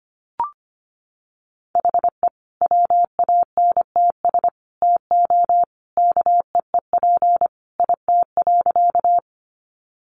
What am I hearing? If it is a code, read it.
Morse code, heard as a keyed tone, about 25 wpm, 700 Hz. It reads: HE WANTS TO KEEP IT.
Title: Additional Easy Phrases 25wpm